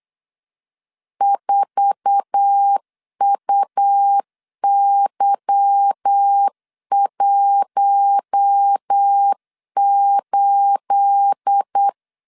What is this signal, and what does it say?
Morse code, heard as a keyed tone, about 8 wpm, 800 Hz. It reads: 4UY18
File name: CW / DET / 1 9 1500C S A 8wpm 5637